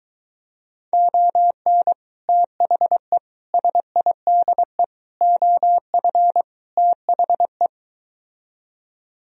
Morse code, heard as a keyed tone, about 23 wpm, 700 Hz. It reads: ON THE SIDE OF THE